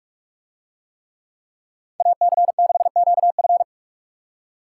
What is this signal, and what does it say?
Morse code, heard as a keyed tone, about 44 wpm, 700 Hz. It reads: AC6QF